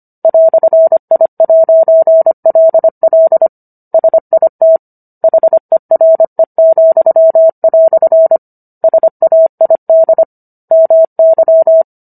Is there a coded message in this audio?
Yes: Morse code from 0.2 to 11.8 s, heard as a keyed tone, about 25 wpm, 650 Hz. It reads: "I'LL SIT HERE," SAID MY